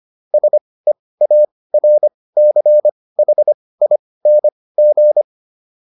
Morse code, heard as a keyed tone, about 25 wpm, spaced as for 18 wpm, 600 Hz. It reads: SEARCHING